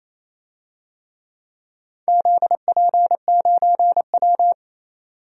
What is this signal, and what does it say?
Morse code, heard as a keyed tone, about 27 wpm, 690 Hz. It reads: ZP9W
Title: DXCC-100 Call Signs - 4 Characters - 28wpm